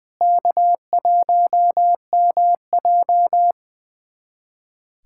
Morse code, heard as a keyed tone, about 20 wpm, 700 Hz. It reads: K1MJ
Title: USA Callsign DE Callsign 20wpm